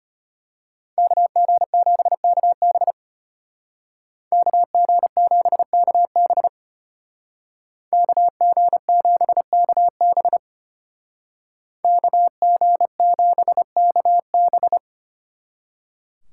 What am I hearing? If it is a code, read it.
Morse code, heard as a keyed tone, about 33 wpm, 700 Hz. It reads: KG7KB KG7KB KG7KB KG7KB